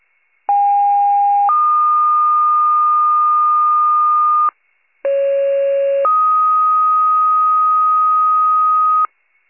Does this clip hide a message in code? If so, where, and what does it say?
Yes: two-tone page from 0.5 to 9.1 s, heard as a one-second tone, then a three-second tone, about 810 then 1220 Hz, about 550 then 1250 Hz.